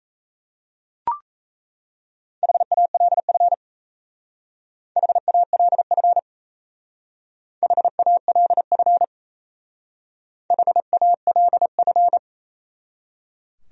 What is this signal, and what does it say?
Morse code, heard as a keyed tone, about 34 wpm, 700 Hz. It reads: HALF HALF HALF HALF
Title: Speed-Racing Top 500 Words 28wpm